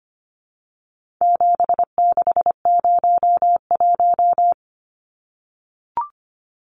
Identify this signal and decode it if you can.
Morse code, heard as a keyed tone, about 25 wpm, 700 Hz. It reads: 7601
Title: Numbers - 4-Digit 25wpm